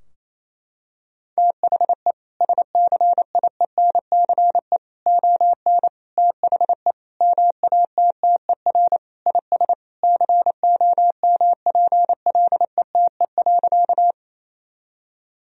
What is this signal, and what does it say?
Morse code, heard as a keyed tone, about 28 wpm, 700 Hz. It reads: THE SCIENCE ON THE MATTER IS COMPLETE.
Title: Sentences from Top 1000 Words - Review Part 2 - 28wpm